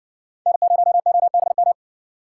Morse code, heard as a keyed tone, about 60 wpm, 700 Hz. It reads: N0OBM